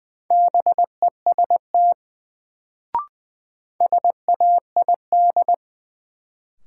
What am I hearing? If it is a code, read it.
Morse code, heard as a keyed tone, about 20 wpm, 700 Hz. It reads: BEST SAID
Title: Sets of 2 Words - New Word Drill - 20wpm